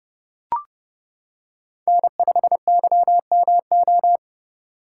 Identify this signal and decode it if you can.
Morse code, heard as a keyed tone, about 28 wpm, 700 Hz. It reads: N5YMO